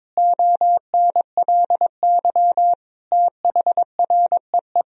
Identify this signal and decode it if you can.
Morse code, heard as a keyed tone, about 22 wpm, 700 Hz. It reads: ONLY THREE
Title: Sentences from Top 1000 Words - Review Part 1 - 22wpm